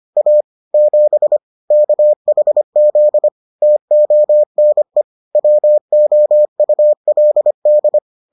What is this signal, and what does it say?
Morse code, heard as a keyed tone, about 25 wpm, 600 Hz. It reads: A 7 KHZ TONE WOULD